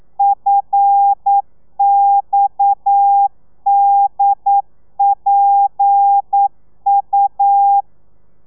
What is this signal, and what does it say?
Morse code, heard as a keyed tone, about 9 wpm, 800 Hz. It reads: FXDPU
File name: telegrafia / i2tfj / Lez7 40